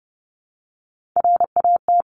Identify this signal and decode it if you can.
Morse code, heard as a keyed tone, about 30 wpm, 700 Hz. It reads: RAT
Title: Top Three Letter Combinations in Words 30wpm